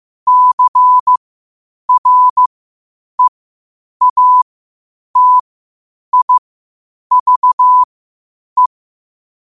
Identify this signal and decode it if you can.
Morse code, heard as a keyed tone, about 15 wpm, spaced as for 8 wpm, 1000 Hz. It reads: CREATIVE
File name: qotdfiles / 8wpm